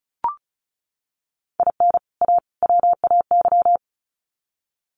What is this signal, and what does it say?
Morse code, heard as a keyed tone, about 34 wpm, 700 Hz. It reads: IN A WAY
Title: Sets of 3 Words 35wpm